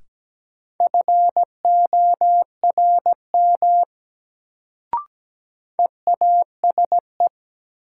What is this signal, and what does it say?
Morse code, heard as a keyed tone, about 17 wpm, 700 Hz. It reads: FORM EASE
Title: Sentences from Top 1000 Words - Review Part 1 - New Word Drill - 17wpm